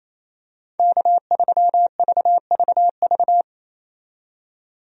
Morse code, heard as a keyed tone, about 28 wpm, 700 Hz. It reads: K3VVV
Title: USA Call Signs - 5 Characters - 28wpm